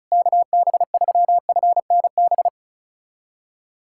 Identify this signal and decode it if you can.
Morse code, heard as a keyed tone, about 35 wpm, 700 Hz. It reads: KB3FNB
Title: USA Callsign DE Callsign - 6 Characters - 35wpm